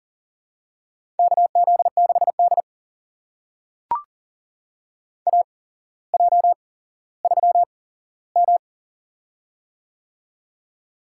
Morse code, heard as a keyed tone, about 40 wpm, spaced as for 17 wpm, 700 Hz. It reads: KZ6D AJ3M
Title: USA Call Signs - 4 Characters - 8x Character Spacing - 40wpm